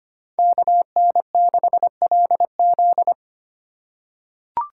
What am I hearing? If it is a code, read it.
Morse code, heard as a keyed tone, about 25 wpm, 700 Hz. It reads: KN6LZ
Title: USA Call Signs - 5 Characters - 3x Character Spacing - 25wpm